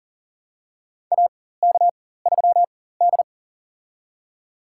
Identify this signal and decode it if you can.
Morse code, heard as a keyed tone, about 37 wpm, spaced as for 18 wpm, 700 Hz. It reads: AK3D
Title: USA Call Signs - 4 Characters - 4x Character Spacing - 40wpm